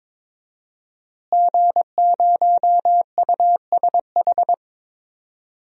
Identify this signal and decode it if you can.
Morse code, heard as a keyed tone, about 22 wpm, 700 Hz. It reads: G0USH
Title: DXCC-100 Call Signs - 5 Characters - Repeated 2x - 22wpm